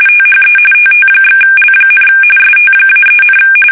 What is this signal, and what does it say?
RTTY, about 45.45 baud. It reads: THE TT/L, ST-3,
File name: fsk850 / teletype-RTTY